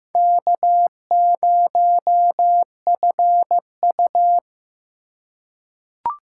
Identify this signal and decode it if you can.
Morse code, heard as a keyed tone, about 15 wpm, 700 Hz. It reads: K0FU
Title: USA Call Signs - 4x Character Spacing - 15wpm